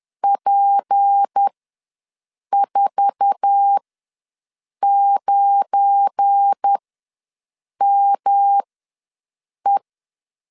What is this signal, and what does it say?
Morse code, heard as a keyed tone, about 10 wpm, spaced as for 6 wpm, 800 Hz. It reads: P49ME
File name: LEZIONI / 9 28 ws 10wpm 4124